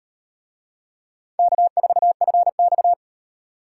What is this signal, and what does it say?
Morse code, heard as a keyed tone, about 33 wpm, 700 Hz. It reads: K4FX